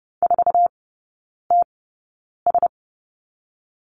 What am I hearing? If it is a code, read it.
Morse code, heard as a keyed tone, about 30 wpm, 700 Hz. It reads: 4 T S
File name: Lesson 19e - Letter w - Three Characters 30wpm